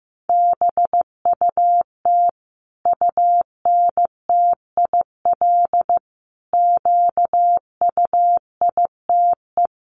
Morse code, heard as a keyed tone, about 15 wpm, 700 Hz. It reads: BUT UNTIL QUITE
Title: Sentences from Top 1200 Words 15wpm